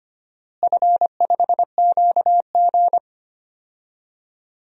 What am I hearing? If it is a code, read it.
Morse code, heard as a keyed tone, about 25 wpm, 700 Hz. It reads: F5QG
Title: DXCC-100 Call Signs - 4 Characters - Repeated 2x - 25wpm